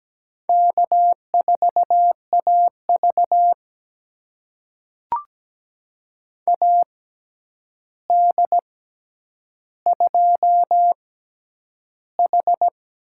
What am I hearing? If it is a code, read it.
Morse code, heard as a keyed tone, about 17 wpm, spaced as for 9 wpm, 700 Hz. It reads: K4AV AD2H
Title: USA Call Signs - 4 Characters - 6x Character Spacing - 17wpm